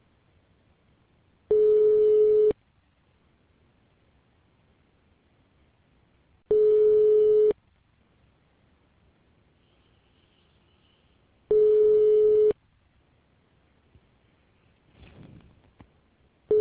Ringback tone (Europe, 425 Hz).